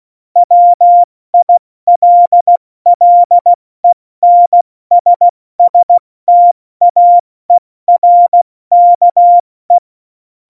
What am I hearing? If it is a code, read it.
Morse code, heard as a keyed tone, about 16 wpm, 700 Hz. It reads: WILLENSSTAERKE
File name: Morsekurs / Witze / Witz 11. Juni 2019, 70 F80